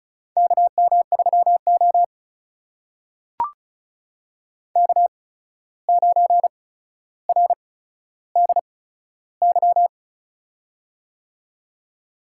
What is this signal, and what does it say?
Morse code, heard as a keyed tone, about 36 wpm, spaced as for 14 wpm, 700 Hz. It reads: KM3O K9RDY
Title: USA Call Signs - 8x Character Spacing - 35wpm